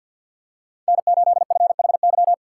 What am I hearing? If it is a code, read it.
Morse code, heard as a keyed tone, about 50 wpm, 700 Hz. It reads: N8FHY
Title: USA Call Signs - 5 Characters - 3x Character Spacing - 50wpm